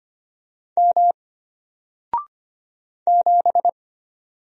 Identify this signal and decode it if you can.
Morse code, heard as a keyed tone, about 25 wpm, 700 Hz. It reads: M 7
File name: Single-Letter-Number-25wpm